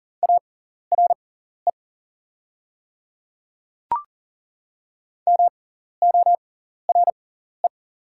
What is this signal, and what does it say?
Morse code, heard as a keyed tone, about 40 wpm, spaced as for 14 wpm, 690 Hz. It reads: ARE MORE
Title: 6x Character Spacing - Top 200 Words - 40wpm